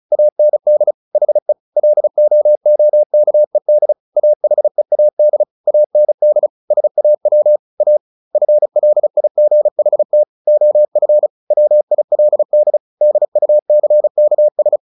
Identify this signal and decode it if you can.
Morse code, heard as a keyed tone, about 34 wpm, 600 Hz. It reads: AND HE LOOKED AHEAD AND SAW A FLIGHT OF WILD DUCKS
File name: the old man and the sea / the old man and the sea 1 35wpm